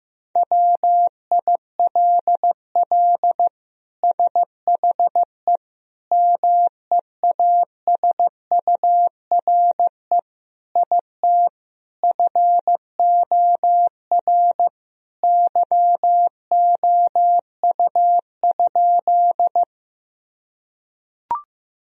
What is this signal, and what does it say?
Morse code, heard as a keyed tone, about 15 wpm, 700 Hz. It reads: WILL SHE MEASURE IT FOR YOU?
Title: Sentences from Top 1000 Words - Review Part 2 - 15wpm